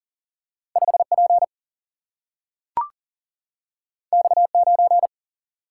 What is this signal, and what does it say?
Morse code, heard as a keyed tone, about 40 wpm, 700 Hz. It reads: 5 P X 9
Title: Single-Letter Single-Number 40wpm